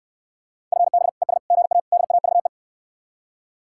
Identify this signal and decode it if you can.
Morse code, heard as a keyed tone, about 69 wpm, 700 Hz. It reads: FB ES GA DAVE